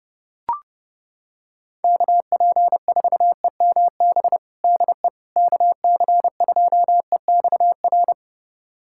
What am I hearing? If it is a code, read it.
Morse code, heard as a keyed tone, about 30 wpm, 700 Hz. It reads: KP4EMB DE KC2EXR
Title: USA Callsign DE Callsign 30wpm